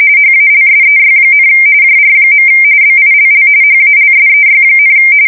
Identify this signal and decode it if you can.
RTTY, about 45.45 baud. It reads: HALSEY, JR. BILL HALSEY,
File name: fsk170 / USS-New-Jersey